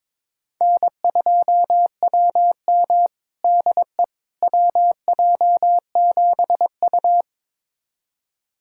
Morse code, heard as a keyed tone, about 22 wpm, 700 Hz. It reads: N2WM DE WJ7U
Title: USA Callsign DE Callsign - 4 Characters - 22wpm